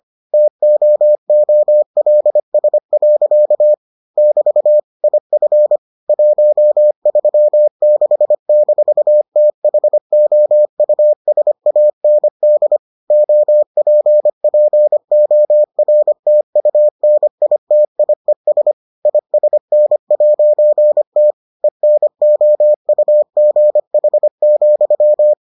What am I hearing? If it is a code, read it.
Morse code, heard as a keyed tone, about 25 wpm, 600 Hz. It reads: TOOLS. <BT> IF 136-THOUSAND OPPORTUNITIES ISN'T ENOUGH,